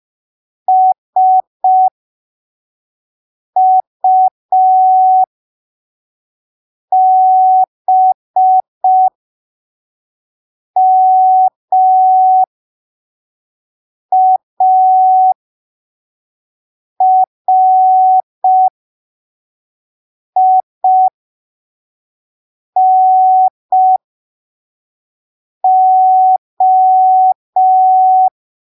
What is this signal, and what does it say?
Morse code, heard as a keyed tone, about 5 wpm, spaced as for 3 wpm, 750 Hz. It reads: SUBMARINO